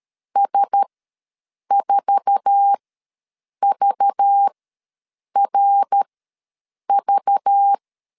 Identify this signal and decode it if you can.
Morse code, heard as a keyed tone, about 13 wpm, spaced as for 7 wpm, 800 Hz. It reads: S4VRV
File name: LEZIONI / 8 YJ45 / 8 YJ4 ws 12wpm 3249